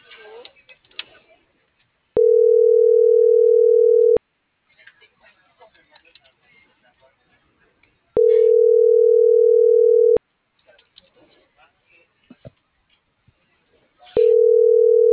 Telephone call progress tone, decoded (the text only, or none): ringback tone